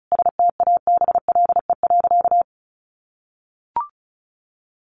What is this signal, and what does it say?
Morse code, heard as a keyed tone, about 34 wpm, 700 Hz. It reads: STABLE.
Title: Sentences from Top 1500 Words 35wpm